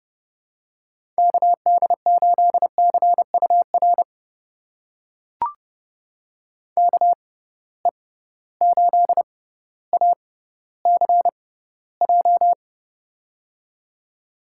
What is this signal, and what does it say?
Morse code, heard as a keyed tone, about 30 wpm, spaced as for 15 wpm, 700 Hz. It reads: KD8CUR KE8ACJ
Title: USA Call Signs - 6 Characters - 6x Character Spacing - 30wpm